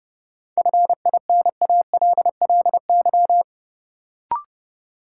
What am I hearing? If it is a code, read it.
Morse code, heard as a keyed tone, about 30 wpm, 700 Hz. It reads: FINALLY
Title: Sentences from Top 1000 Words - New Word Drill - 30wpm